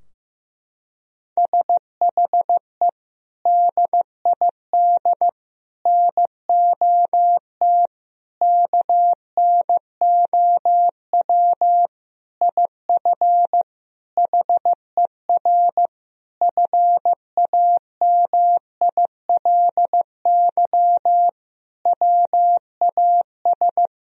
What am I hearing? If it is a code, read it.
Morse code, heard as a keyed tone, about 15 wpm, 700 Hz. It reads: SHE DID NOT KNOW IF HER FAMILY WAS
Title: Sentences from Top 1000 Words - Review Part 4 - 15wpm